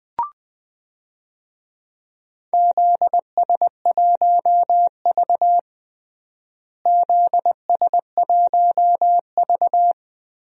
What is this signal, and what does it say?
Morse code, heard as a keyed tone, about 20 wpm, 700 Hz. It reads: ZS1V ZS1V
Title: DXCC-100 Call Signs - 4 Characters - Repeated 2x - 20wpm